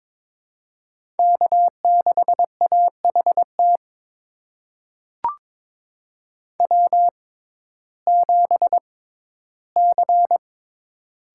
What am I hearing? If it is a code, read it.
Morse code, heard as a keyed tone, about 22 wpm, spaced as for 14 wpm, 700 Hz. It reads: K6AHT W7C
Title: USA Call Signs - 5 Characters - 6x Character Spacing - 22wpm